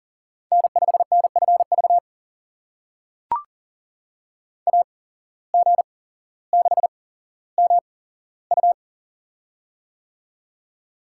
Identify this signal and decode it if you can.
Morse code, heard as a keyed tone, about 40 wpm, spaced as for 17 wpm, 700 Hz. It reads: N5NFV AG6MU
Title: USA Call Signs - 5 Characters - 8x Character Spacing - 40wpm